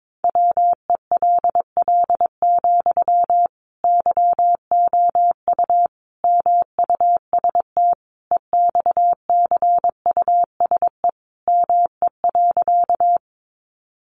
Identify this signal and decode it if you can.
Morse code, heard as a keyed tone, about 22 wpm, 700 Hz. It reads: WELL, YOU MUST EXCUSE ME.
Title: Sentences from Top 700 Words 22wpm